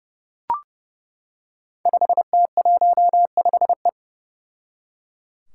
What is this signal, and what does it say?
Morse code, heard as a keyed tone, about 30 wpm, 700 Hz. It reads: 5T15E
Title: Five Challenging Letter-Number Combinations 30wpm